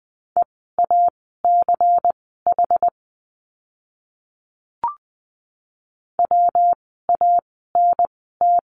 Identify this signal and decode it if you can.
Morse code, heard as a keyed tone, about 20 wpm, spaced as for 15 wpm, 700 Hz. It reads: EACH WANT
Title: 192 Lesson 31f - Letter v - Words 20wpm